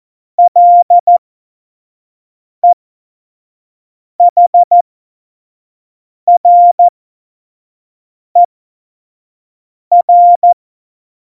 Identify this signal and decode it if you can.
Morse code, heard as a keyed tone, about 14 wpm, spaced as for 5 wpm, 700 Hz. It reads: LEHRER
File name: Morsekurs / Witze / Witz 07. Mai 2019, 25 F70